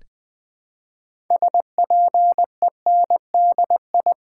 Morse code, heard as a keyed tone, about 20 wpm, 700 Hz. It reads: SPENDI
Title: Build A Word - 2 Letters At A Time - 8-Letter Words - 20wpm